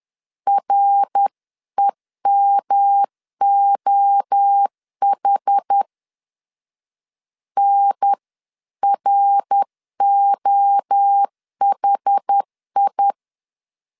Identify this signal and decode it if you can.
Morse code, heard as a keyed tone, about 10 wpm, 800 Hz. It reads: REMOH NROHI
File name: 10wpm